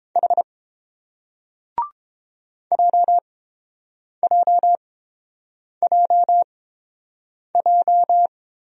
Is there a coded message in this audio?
Yes: Morse code from 0.2 to 8.3 s, heard as a keyed tone, about 30 wpm, spaced as for 7 wpm, 700 Hz. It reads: H JJJJ